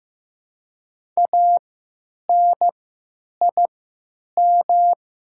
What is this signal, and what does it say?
Morse code, heard as a keyed tone, about 15 wpm, spaced as for 9 wpm, 700 Hz. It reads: ANIM
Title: 3x Character Spacing - Top 200 Words - 15wpm